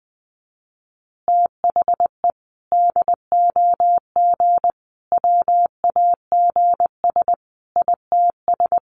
Morse code, heard as a keyed tone, about 20 wpm, 700 Hz. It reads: THE DOG WAGS ITS
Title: Easy Phrases 20wpm